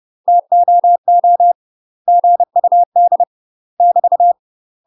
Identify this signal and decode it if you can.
Morse code, heard as a keyed tone, about 30 wpm, 700 Hz. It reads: TOO GUD <BT>
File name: UK MASTER QSOs TRAINING / 30WPM / UK MASTER QSOs 0001